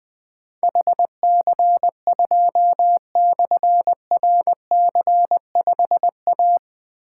Morse code, heard as a keyed tone, about 20 wpm, 700 Hz. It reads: HC2/RC5A
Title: Difficult Call Signs 20wpm